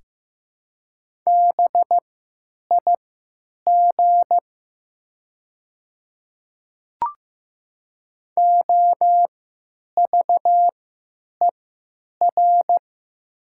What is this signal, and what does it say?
Morse code, heard as a keyed tone, about 15 wpm, spaced as for 9 wpm, 700 Hz. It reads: BIG OVER
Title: 3x Character Spacing - Top 200 Words - 15wpm